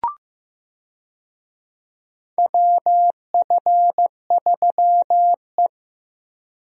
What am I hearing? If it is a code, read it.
Morse code, heard as a keyed tone, about 15 wpm, 700 Hz. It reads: WF3E
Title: USA Call Signs - 4 Characters - Repeated 2x - 15wpm